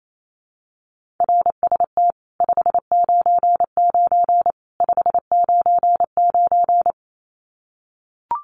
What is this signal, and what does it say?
Morse code, heard as a keyed tone, about 28 wpm, 700 Hz. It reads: RST 599 599
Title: Common QSO Phrases 28wpm